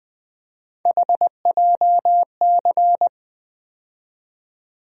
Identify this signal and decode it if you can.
Morse code, heard as a keyed tone, about 20 wpm, 700 Hz. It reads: HJC
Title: USA Call Sign Postfixes - 3 Letters 20wpm